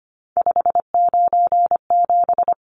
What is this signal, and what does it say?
Morse code, heard as a keyed tone, about 25 wpm, 700 Hz. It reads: 597
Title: Numbers - 3-Digit 25wpm